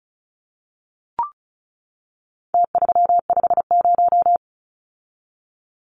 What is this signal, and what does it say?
Morse code, heard as a keyed tone, about 34 wpm, 700 Hz. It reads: T350
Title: Numbers - Contest Style - 4-Digit 35wpm